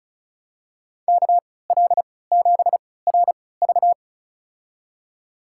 Morse code, heard as a keyed tone, about 36 wpm, spaced as for 20 wpm, 690 Hz. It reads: KL7RV
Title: USA Call Signs - 5 Characters - 3x Character Spacing - 35wpm